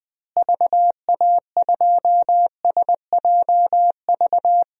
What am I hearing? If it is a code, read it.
Morse code, heard as a keyed tone, about 20 wpm, 700 Hz. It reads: VA2SJV
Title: DXCC-100 Call Signs - 6 Characters - Repeated 2x - 20wpm